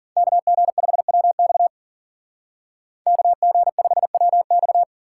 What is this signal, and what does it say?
Morse code, heard as a keyed tone, about 52 wpm, 700 Hz. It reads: KG5WX KG5WX